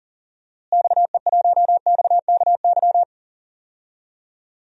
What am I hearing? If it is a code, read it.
Morse code, heard as a keyed tone, about 40 wpm, 700 Hz. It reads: XE1XKY
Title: DXCC-100 Call Signs - 6 Characters - Repeated 3x - 40wpm